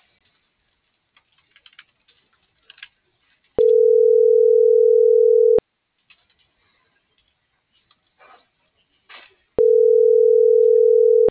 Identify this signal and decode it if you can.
Ringback tone (North America), 440 + 480 Hz.